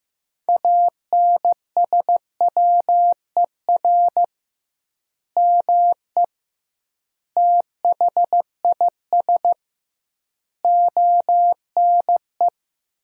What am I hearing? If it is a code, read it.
Morse code, heard as a keyed tone, about 15 wpm, 700 Hz. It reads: ANSWER ME THIS ONE
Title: 2x Word-Spacing - Sentences from Top 400 Words - 15wpm